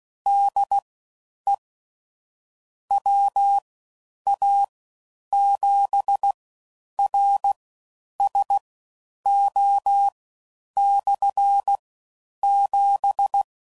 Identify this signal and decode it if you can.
Morse code, heard as a keyed tone, about 16 wpm, spaced as for 9 wpm, 800 Hz. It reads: DE WA7RSO/7
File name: Basic Sets / QST de WA7RSO / All Dits